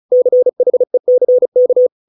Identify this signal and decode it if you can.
Morse code, heard as a keyed tone, about 36 wpm, 500 Hz. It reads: CHECK